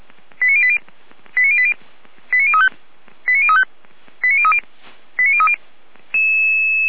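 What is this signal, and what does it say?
Five-tone selective call (ZVEI), heard as quick runs of five tones, tones of about 70 ms. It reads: 89080 89080 89035 89035 89030 89030
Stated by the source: Five-tone ZVEI selective call 89084, received by SDR.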